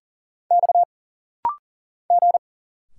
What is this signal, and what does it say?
Morse code, heard as a keyed tone, about 40 wpm, 700 Hz. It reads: X G
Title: Single Letter-Number Alt - Rapid-Fire 40wpm